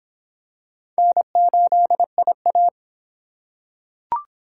Morse code, heard as a keyed tone, about 26 wpm, 700 Hz. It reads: N8IA